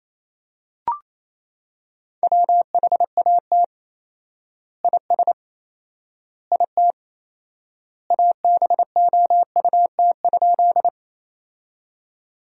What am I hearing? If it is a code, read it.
Morse code, heard as a keyed tone, about 28 wpm, 700 Hz. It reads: WHAT IS IT ABOUT?
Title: 4x Word-Spacing - Sentences from Top 100 Words - 28wpm